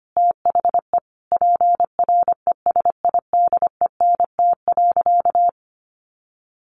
Morse code, heard as a keyed tone, about 25 wpm, 700 Hz. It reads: THE PRESIDENT.
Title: Sentences from Top 1500 Words 25wpm